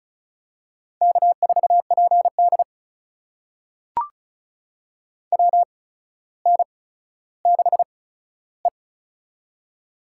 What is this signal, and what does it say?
Morse code, heard as a keyed tone, about 36 wpm, spaced as for 15 wpm, 700 Hz. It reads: K4PD WN6E
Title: USA Call Signs - 4 Characters - 8x Character Spacing - 35wpm